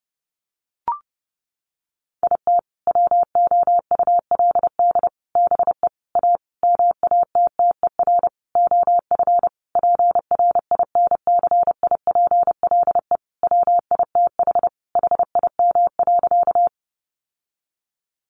Morse code, heard as a keyed tone, about 30 wpm, 700 Hz. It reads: IT WOULD BE A MATTER OF PRINCIPLE WITH HIM.
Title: Sentences from Top 1400 Words 30wpm